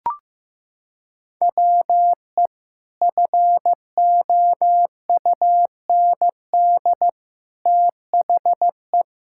Morse code, heard as a keyed tone, about 15 wpm, 700 Hz. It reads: WE FOUND THE
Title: Sentences from Top 1000 Words - Review Part 5 - 15wpm